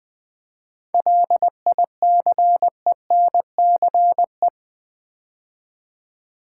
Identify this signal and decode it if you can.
Morse code, heard as a keyed tone, about 20 wpm, 700 Hz. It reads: LICENCE
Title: 7 Letter Words - First 3 Letters Given - 20wpm